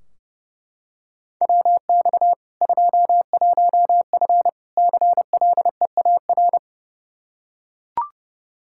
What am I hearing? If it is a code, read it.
Morse code, heard as a keyed tone, about 30 wpm, 700 Hz. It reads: WX 21F CLEAR
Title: Practice QSO - Exchange 2 - Info - 30wpm